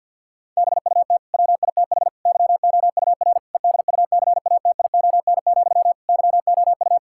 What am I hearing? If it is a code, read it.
Morse code, heard as a keyed tone, about 50 wpm, 700 Hz. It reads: BUT WITH YOUR EDUCATION, YOU
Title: Sentences from Top 1000 Words - Review Part 6 - 50wpm